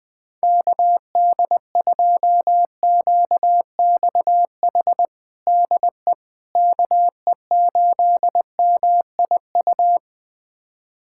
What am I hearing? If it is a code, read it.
Morse code, heard as a keyed tone, about 20 wpm, 700 Hz. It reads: KD2QXH DE KE8MIU
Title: USA Callsign DE Callsign - 6 Characters - 20wpm